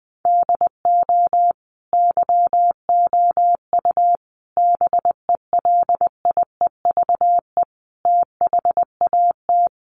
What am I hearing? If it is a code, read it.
Morse code, heard as a keyed tone, about 20 wpm, 700 Hz. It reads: DO YOU BELIEVE THAT
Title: Sentences from Top 1500 Words 20wpm